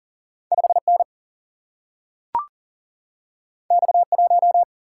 Morse code, heard as a keyed tone, about 39 wpm, 700 Hz. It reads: 5N X1
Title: Single-Letter Single-Number 40wpm